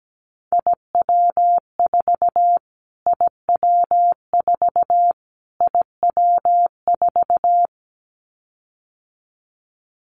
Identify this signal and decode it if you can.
Morse code, heard as a keyed tone, about 17 wpm, 700 Hz. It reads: IW4 IW4 IW4
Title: Callsign Prefixes for Top 25 DXCC - Repeated 3x - 17wpm